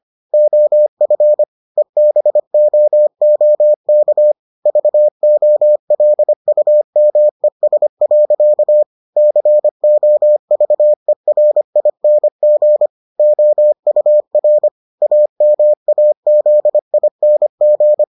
Morse code, heard as a keyed tone, about 25 wpm, 600 Hz. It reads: OF EBOOK VOLUMES. COVERING OUR AMAZING